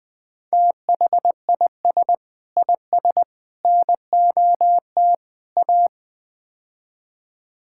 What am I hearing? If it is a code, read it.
Morse code, heard as a keyed tone, about 20 wpm, 700 Hz. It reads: THIS IS NOT A
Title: Build A Phrase - 4-Word Phrases - 20wpm